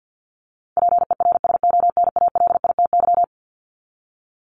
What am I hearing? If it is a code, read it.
Morse code, heard as a keyed tone, about 50 wpm, 700 Hz. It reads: PERSONALITY
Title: Top 1251-1500 Words 50wpm